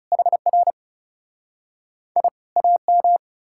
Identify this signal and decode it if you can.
Morse code, heard as a keyed tone, about 40 wpm, 700 Hz. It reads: HR I AM